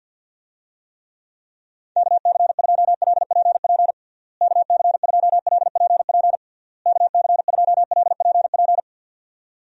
Morse code, heard as a keyed tone, about 50 wpm, 700 Hz. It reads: KC2LPP KC2LPP KC2LPP
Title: USA Call Signs - 6 Characters - Repeated 3x - 50wpm